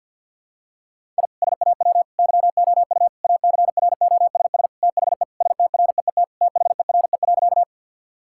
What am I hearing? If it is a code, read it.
Morse code, heard as a keyed tone, about 52 wpm, 700 Hz. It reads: I SAW YOU ACROSS THE STREET THERE.
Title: Sentences from Top 1000 Words - Review Part 5 - 50wpm